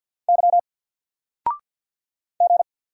Morse code, heard as a keyed tone, about 50 wpm, 700 Hz. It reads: Y G
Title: Single Letter-Number Alt 50wpm